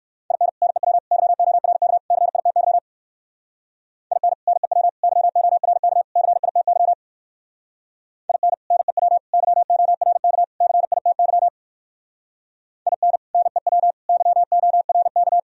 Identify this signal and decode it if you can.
Morse code, heard as a keyed tone, about 58 wpm, 700 Hz. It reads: IN NEW YORK CITY IN NEW YORK CITY IN NEW YORK CITY IN NEW YORK